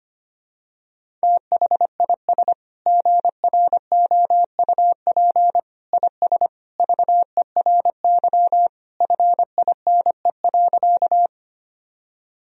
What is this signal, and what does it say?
Morse code, heard as a keyed tone, about 25 wpm, 700 Hz. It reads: THIS GROUP IS VERY FINE.
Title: Sentences from Top 1000 Words - Review Part 1 - 25wpm